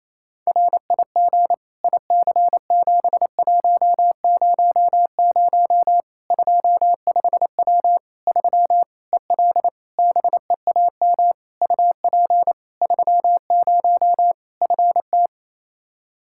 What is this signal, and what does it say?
Morse code, heard as a keyed tone, about 27 wpm, 700 Hz. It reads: RIG IC7100 25W 3 EL BEAM UP 30 FT
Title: Practice QSO - Exchange 2 - Info - 28wpm